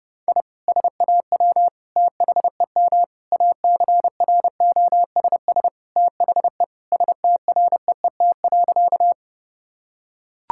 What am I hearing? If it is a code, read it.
Morse code, heard as a keyed tone, about 29 wpm, 700 Hz. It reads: I SAW THEM ACROSS THE STREET.
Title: Sentences from Top 1000 Words - Review Part 5 - 30wpm